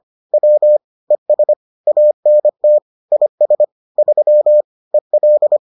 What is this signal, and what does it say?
Morse code, heard as a keyed tone, about 25 wpm, 600 Hz. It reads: W ES ANT IS 3 EL